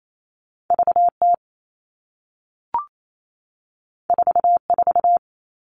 Morse code, heard as a keyed tone, about 28 wpm, 700 Hz. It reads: VT 44
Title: All Cut Numbers - 2-Digit 28wpm